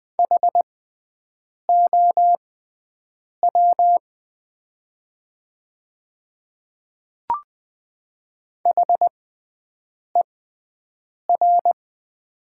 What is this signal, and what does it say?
Morse code, heard as a keyed tone, about 20 wpm, spaced as for 7 wpm, 700 Hz. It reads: HOW HER
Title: 6x Character Spacing - Top 200 Words - 20wpm